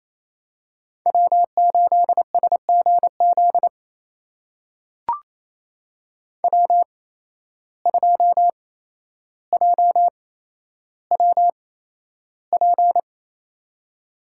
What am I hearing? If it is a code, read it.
Morse code, heard as a keyed tone, about 28 wpm, spaced as for 12 wpm, 700 Hz. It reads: W8SGZ W2JWP
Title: USA Call Signs - 5 Characters - 8x Character Spacing - 28wpm